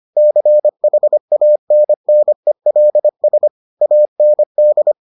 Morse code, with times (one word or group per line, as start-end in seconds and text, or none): 0.2-3.5 CHANNELS
3.8-4.9 AND